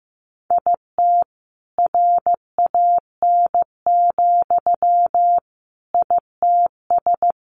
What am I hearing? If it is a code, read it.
Morse code, heard as a keyed tone, about 15 wpm, 700 Hz. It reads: IT RAN, ITS
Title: A Sound Of Thunder 15wpm